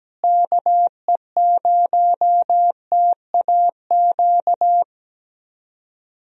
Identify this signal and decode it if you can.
Morse code, heard as a keyed tone, about 17 wpm, 700 Hz. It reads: KE0TAQ
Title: USA Call Signs - 6 Characters - Repeated 2x - 17wpm